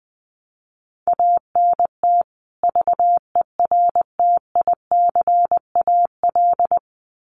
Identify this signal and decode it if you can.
Morse code, heard as a keyed tone, about 20 wpm, 700 Hz. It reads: ANT VERTICAL
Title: Common QSO Phrases 20wpm